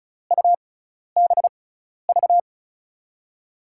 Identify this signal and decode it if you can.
Morse code, heard as a keyed tone, about 36 wpm, spaced as for 12 wpm, 700 Hz. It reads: UBV
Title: USA Call Signs - 5 Characters - 6x Character Spacing - 35wpm